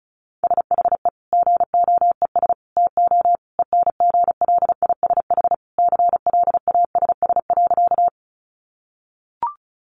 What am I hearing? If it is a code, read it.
Morse code, heard as a keyed tone, about 34 wpm, 700 Hz. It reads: SHE GOES TO ENGLISH CLASS.
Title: Additional Easy Phrases 35wpm